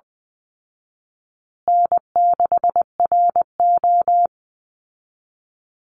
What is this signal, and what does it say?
Morse code, heard as a keyed tone, about 20 wpm, 700 Hz. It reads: N6RO
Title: Easy Call Signs Repeated 3x - 20wpm